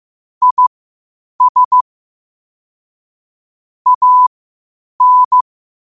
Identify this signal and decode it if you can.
Morse code, heard as a keyed tone, about 15 wpm, spaced as for 8 wpm, 1000 Hz. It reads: IS AN
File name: qotdfiles / 8wpm